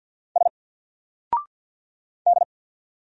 Morse code, heard as a keyed tone, about 48 wpm, 700 Hz. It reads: S D
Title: Single Character (Letter:Number:Punctuation:Prosign) Alt 50wpm